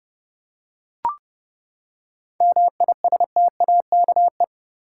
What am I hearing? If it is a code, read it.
Morse code, heard as a keyed tone, about 30 wpm, 700 Hz. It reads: MISTAKE
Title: Seven Letter Words - 30wpm